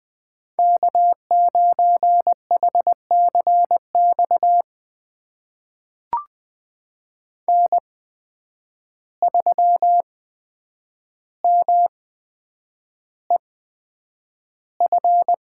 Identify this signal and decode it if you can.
Morse code, heard as a keyed tone, about 20 wpm, spaced as for 9 wpm, 700 Hz. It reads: K9HCX N3MEF
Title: USA Call Signs - 5 Characters - 8x Character Spacing - 20wpm